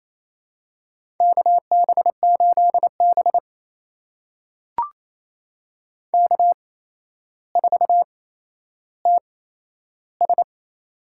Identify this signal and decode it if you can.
Morse code, heard as a keyed tone, about 27 wpm, spaced as for 12 wpm, 700 Hz. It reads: KB8B K4TS
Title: USA Call Signs - 8x Character Spacing - 28wpm